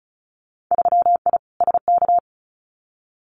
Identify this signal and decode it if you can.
Morse code, heard as a keyed tone, about 36 wpm, 700 Hz. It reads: 3I SK
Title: Winter Field Day Exchanges Encore 35wpm